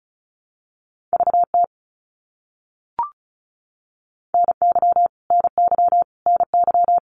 Morse code, heard as a keyed tone, about 34 wpm, 700 Hz. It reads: VT NY NY NY
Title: US States Repeated 3x 35wpm